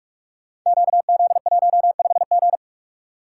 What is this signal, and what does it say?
Morse code, heard as a keyed tone, about 42 wpm, 700 Hz. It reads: QZ15G
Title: Code Groups - 5 Letter:Numbers - 45wpm